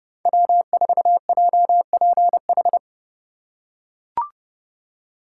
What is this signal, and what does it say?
Morse code, heard as a keyed tone, about 30 wpm, 700 Hz. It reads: W4JPH
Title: K1USN SST QSO Practice - 30wpm